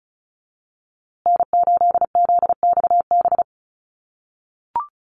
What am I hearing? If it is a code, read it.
Morse code, heard as a keyed tone, about 34 wpm, 700 Hz. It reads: N8ZXB
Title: USA Call Signs Repeated 3x 35wpm